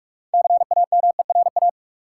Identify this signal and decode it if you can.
Morse code, heard as a keyed tone, about 46 wpm, 700 Hz. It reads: CAMERA
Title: Sentences from Top 1000 Words - Review Part 5 - New Word Drill - 45wpm